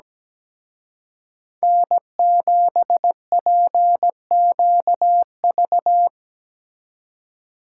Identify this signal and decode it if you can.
Morse code, heard as a keyed tone, about 20 wpm, 700 Hz. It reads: N7PQV
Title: Speed Racing - USA Call Signs 17wpm